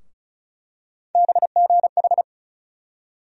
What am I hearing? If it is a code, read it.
Morse code, heard as a keyed tone, about 34 wpm, 700 Hz. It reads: BGH
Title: USA Call Sign Postfixes - 3 Letters 35wpm